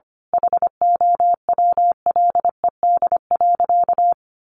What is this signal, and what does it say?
Morse code, heard as a keyed tone, about 25 wpm, 700 Hz. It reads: HOWLED.
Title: A Sound Of Thunder 25wpm